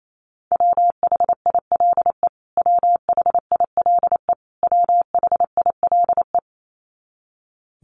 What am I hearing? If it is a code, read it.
Morse code, heard as a keyed tone, about 28 wpm, 700 Hz. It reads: WHILE WHILE WHILE
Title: Top 500 Words Repeated 28wpm